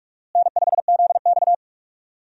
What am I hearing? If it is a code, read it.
Morse code, heard as a keyed tone, about 44 wpm, 700 Hz. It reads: N5ZX
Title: USA Call Signs - 4 Characters - Repeated 3x - 45wpm